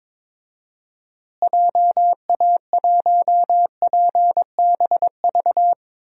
Morse code, heard as a keyed tone, about 22 wpm, 700 Hz. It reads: JA1PBV
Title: DXCC-100 Call Signs - 6 Characters - Repeated 2x - 22wpm